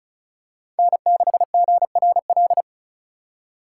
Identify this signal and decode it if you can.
Morse code, heard as a keyed tone, about 36 wpm, 700 Hz. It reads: N6GRL
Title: USA Call Signs - 3x Character Spacing - 35wpm